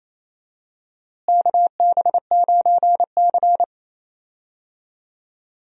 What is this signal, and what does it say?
Morse code, heard as a keyed tone, about 28 wpm, 700 Hz. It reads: KB9C
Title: USA Call Signs - 4 Characters - Repeated 3x - 28wpm